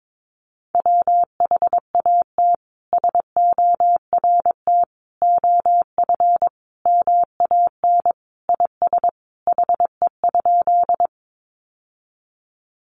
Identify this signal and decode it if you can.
Morse code, heard as a keyed tone, about 22 wpm, 700 Hz. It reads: WHAT SORT OF MAN IS HE?
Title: Sentences from Top 1300 Words 22wpm